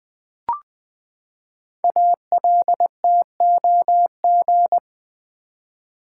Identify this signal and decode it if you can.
Morse code, heard as a keyed tone, about 20 wpm, 700 Hz. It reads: ALTOG
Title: Build A Word - 5 Letters At A Time - 10-Letter Words - 20wpm